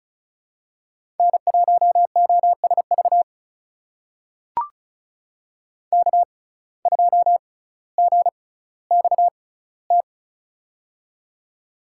Morse code, heard as a keyed tone, about 36 wpm, spaced as for 18 wpm, 700 Hz. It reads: N1OSV K2GXT
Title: USA Call Signs - 5 Characters - 6x Character Spacing - 35wpm